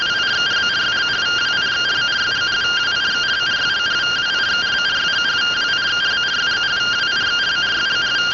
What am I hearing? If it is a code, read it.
Two-tone teleprinter signal, about 170 Hz shift.